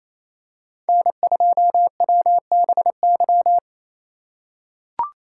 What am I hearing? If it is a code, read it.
Morse code, heard as a keyed tone, about 28 wpm, 700 Hz. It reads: N2WBY
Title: USA Call Signs - 5 Characters - 2x Character Spacing - 28wpm